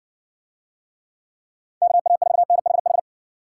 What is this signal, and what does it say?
Morse code, heard as a keyed tone, about 55 wpm, 700 Hz. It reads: KA4NHH